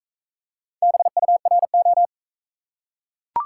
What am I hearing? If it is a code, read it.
Morse code, heard as a keyed tone, about 44 wpm, 700 Hz. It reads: BURO